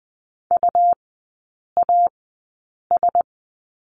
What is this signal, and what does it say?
Morse code, heard as a keyed tone, about 20 wpm, spaced as for 8 wpm, 690 Hz. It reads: UAS
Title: 159 Lesson 26e - Letter g - Three Characters 20wpm